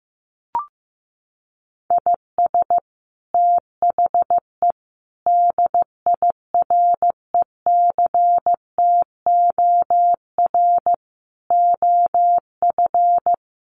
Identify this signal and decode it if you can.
Morse code, heard as a keyed tone, about 15 wpm, 700 Hz. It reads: IS THE DIRECTOR OF
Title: Sets of 5 Words Encore2 15wpm